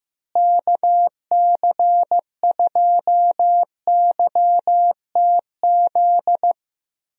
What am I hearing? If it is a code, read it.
Morse code, heard as a keyed tone, about 15 wpm, 700 Hz. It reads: KC2YTZ
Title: USA Call Signs - 6 Characters - Repeated 2x - 15wpm